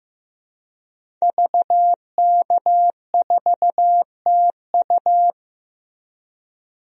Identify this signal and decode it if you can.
Morse code, heard as a keyed tone, about 15 wpm, 700 Hz. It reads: VK4TU
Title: DXCC-100 Call Signs - 5 Characters - Repeated 3x - 15wpm